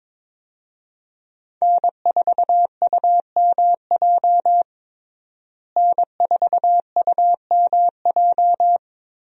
Morse code, heard as a keyed tone, about 21 wpm, 700 Hz. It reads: N4UMJ N4UMJ
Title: USA Call Signs - 5 Characters - Repeated 2x - 22wpm